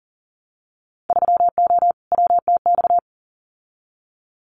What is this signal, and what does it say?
Morse code, heard as a keyed tone, about 38 wpm, 700 Hz. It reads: 3O WTX
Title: Winter Field Day Exchanges Encore 40wpm